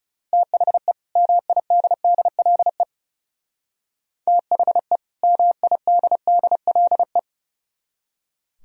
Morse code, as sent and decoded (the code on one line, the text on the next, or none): - .... / -- .. -.. -.. .-.. / - .... / -- .. -.. -.. .-.. .
THE MIDDLE THE MIDDLE